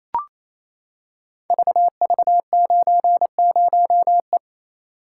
Morse code, heard as a keyed tone, about 27 wpm, 700 Hz. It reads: VV90E
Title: Code Groups - 5 Letter:Numbers - 28wpm